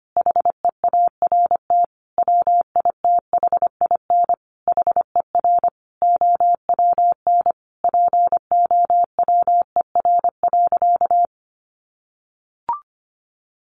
Morse code, heard as a keyed tone, about 25 wpm, 700 Hz. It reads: HEART WITHIN HER OWN POWER.
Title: Sentences from Top 1200 Words 25wpm